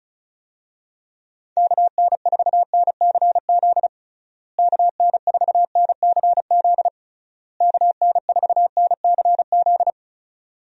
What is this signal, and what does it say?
Morse code, heard as a keyed tone, about 34 wpm, 700 Hz. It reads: KN4NCZ KN4NCZ KN4NCZ
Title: USA Call Signs - 6 Characters - Repeated 3x - 35wpm